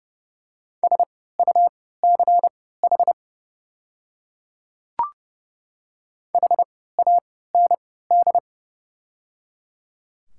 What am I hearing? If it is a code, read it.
Morse code, heard as a keyed tone, about 30 wpm, spaced as for 17 wpm, 700 Hz. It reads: SUCH HAND
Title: 3x Character Spacing - Top 200 Words - 30wpm